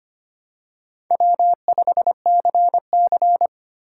Morse code, heard as a keyed tone, about 25 wpm, 700 Hz. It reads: W5CC
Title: USA Call Signs - 6x Character Spacing - 25wpm